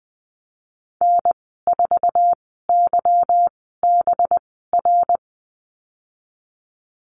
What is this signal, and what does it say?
Morse code, heard as a keyed tone, about 20 wpm, spaced as for 15 wpm, 700 Hz. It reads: N4YBR
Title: 186 Lesson 30g - Letter b - Call Signs 20wpm